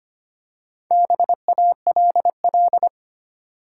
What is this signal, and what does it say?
Morse code, heard as a keyed tone, about 25 wpm, 690 Hz. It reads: BALL
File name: practice / 25wpm / Easy Words with Hints - 25wpm